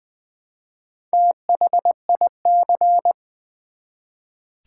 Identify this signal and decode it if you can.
Morse code, heard as a keyed tone, about 20 wpm, 700 Hz. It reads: THIC
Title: Build A Word - 1 Letter At A Time - 5-Letter Words - 20wpm